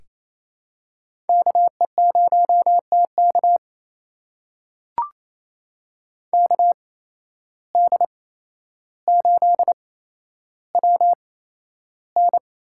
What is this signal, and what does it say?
Morse code, heard as a keyed tone, about 28 wpm, spaced as for 12 wpm, 700 Hz. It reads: KE0TK KD8WN
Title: USA Call Signs - 8x Character Spacing - 28wpm